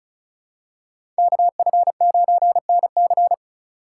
Morse code, heard as a keyed tone, about 36 wpm, 700 Hz. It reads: KF9NC
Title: USA Call Signs - 5 Characters - 3x Character Spacing - 35wpm